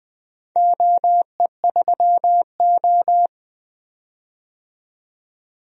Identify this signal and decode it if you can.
Morse code, heard as a keyed tone, about 20 wpm, 700 Hz. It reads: OE3O
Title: DXCC-100 Call Signs - 4 Characters - 20wpm